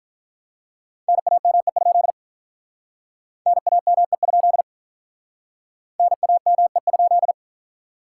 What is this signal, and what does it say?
Morse code, heard as a keyed tone, about 44 wpm, 700 Hz. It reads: NAME? NAME? NAME?